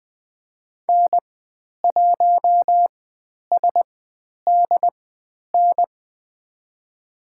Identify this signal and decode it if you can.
Morse code, heard as a keyed tone, about 20 wpm, spaced as for 10 wpm, 700 Hz. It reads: N1SDN
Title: 102 Lesson 17g - Letter u - Call Signs 20wpm